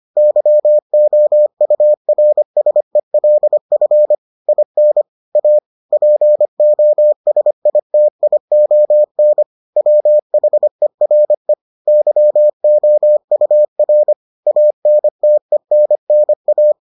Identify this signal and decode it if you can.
Morse code, heard as a keyed tone, about 25 wpm, 600 Hz. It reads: YOURSELF IN A POSITION WHERE YOUR ANTENNA